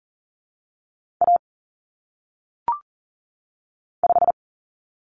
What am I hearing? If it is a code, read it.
Morse code, heard as a keyed tone, about 40 wpm, 700 Hz. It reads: A 5
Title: Common Cut Numbers - 1-Digit 40wpm